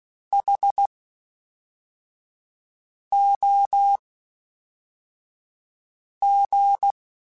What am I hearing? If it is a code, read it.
Morse code, heard as a keyed tone, about 16 wpm, spaced as for 4 wpm, 800 Hz. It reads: HOG
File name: H Prefixes / H mp3s